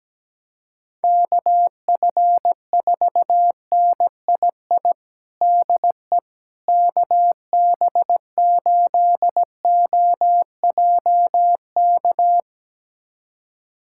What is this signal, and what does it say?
Morse code, heard as a keyed tone, about 17 wpm, 700 Hz. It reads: KF4NII DE KB8OJK
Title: USA Callsign DE Callsign - 6 Characters - 17wpm